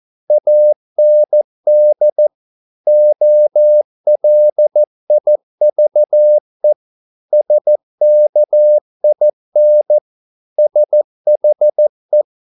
Morse code, heard as a keyed tone, about 14 wpm, 600 Hz. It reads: AND OLIVE SKIN SHE